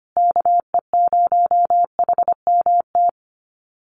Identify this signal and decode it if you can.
Morse code, heard as a keyed tone, about 25 wpm, 700 Hz. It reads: KE0HMT
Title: USA Call Signs Repeated 2x 25wpm